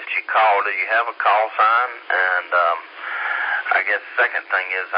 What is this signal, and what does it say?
Emergency Alert System (SAME) tones, heard as harsh buzzing bursts, with speech. It is heard from 0.7 to 3.9 s.